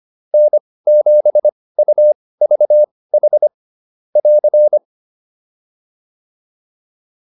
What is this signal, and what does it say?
Morse code, heard as a keyed tone, about 25 wpm, spaced as for 18 wpm, 600 Hz. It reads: N7UVH <AR>